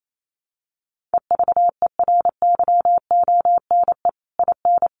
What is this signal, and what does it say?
Morse code, heard as a keyed tone, about 28 wpm, 700 Hz. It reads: EVERYONE IN
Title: Sentences from Top 1500 Words 28wpm